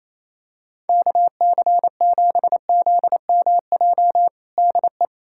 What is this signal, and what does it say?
Morse code, heard as a keyed tone, about 28 wpm, 700 Hz. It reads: KC7ZMJ DE
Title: USA Callsign DE Callsign - 6 Characters - 28wpm